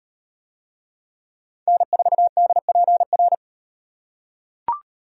Morse code, heard as a keyed tone, about 38 wpm, 700 Hz. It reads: N4DPR